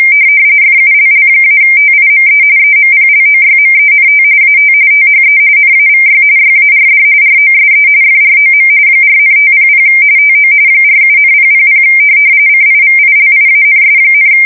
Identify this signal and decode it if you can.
RTTY, about 45.45 baud. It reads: FULL-TIME EMPLOYMENT AT THE NAVY YARD ROSE FROM 4,500 TO 47,000 CIVILIANS,
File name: fsk170 / USS-New-Jersey